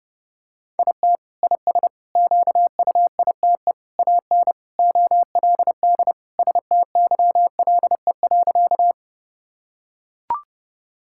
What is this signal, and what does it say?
Morse code, heard as a keyed tone, about 30 wpm, 700 Hz. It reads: IT IS QUITE AN OLD STYLE.
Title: Sentences from Top 1000 Words - Review Part 5 - 30wpm